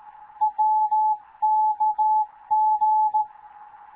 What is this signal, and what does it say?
Morse code, heard as a keyed tone, about 15 wpm, 860 Hz. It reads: WKG
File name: Morze-HA2MN / hangp / 2 11 GYAKORLAT